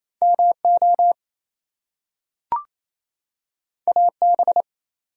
Morse code, heard as a keyed tone, about 28 wpm, 700 Hz. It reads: MO AB
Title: US States and Canadian Provinces 28wpm